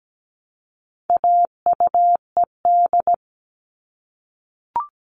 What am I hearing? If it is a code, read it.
Morse code, heard as a keyed tone, about 17 wpm, 700 Hz. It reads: AUED
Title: All Cut Numbers - 4-Digit 17wpm